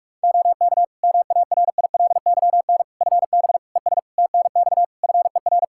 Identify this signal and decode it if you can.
Morse code, heard as a keyed tone, about 46 wpm, 700 Hz. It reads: OK MARILYN FB ES TNX FER
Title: Practice QSO - Exchange 2 - Info - 45wpm